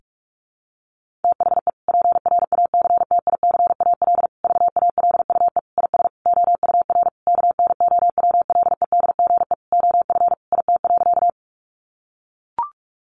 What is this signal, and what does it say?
Morse code, heard as a keyed tone, about 44 wpm, 700 Hz. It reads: THE PRACTICAL VALUE IS OUR KNOWLEDGE OF IT.
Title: Sentences from Top 1400 Words 45wpm